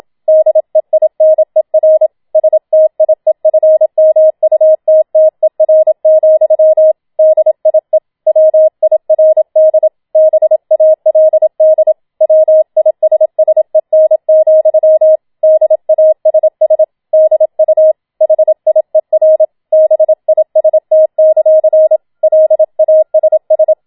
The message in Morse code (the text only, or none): DEINER STIEFMUTTER, DIE WIRD BALD WISSEN, DASS DU HIER BIST; LASS